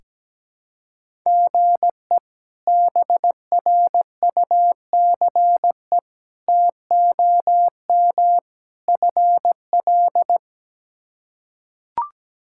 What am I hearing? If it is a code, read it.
Morse code, heard as a keyed tone, about 17 wpm, 700 Hz. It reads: GE BRUCE TOM FL
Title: K1USN SST QSO Practice - 17wpm